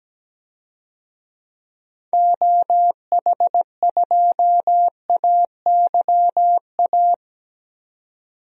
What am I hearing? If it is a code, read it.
Morse code, heard as a keyed tone, about 17 wpm, 700 Hz. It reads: OH2AYA
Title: DXCC-100 Call Signs - 6 Characters - 17wpm